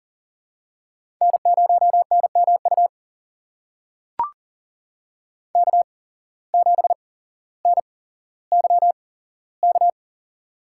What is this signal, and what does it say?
Morse code, heard as a keyed tone, about 40 wpm, spaced as for 17 wpm, 700 Hz. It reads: N0NMU K7NYK
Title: USA Call Signs - 5 Characters - 8x Character Spacing - 40wpm